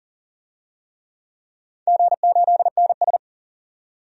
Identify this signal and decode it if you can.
Morse code, heard as a keyed tone, about 40 wpm, 700 Hz. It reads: G8NS
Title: DXCC-100 Call Signs - 4 Characters - 40wpm